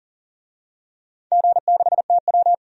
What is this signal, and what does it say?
Morse code, heard as a keyed tone, about 40 wpm, 700 Hz. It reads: G6TW
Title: DXCC-100 Call Signs - 4 Characters - 40wpm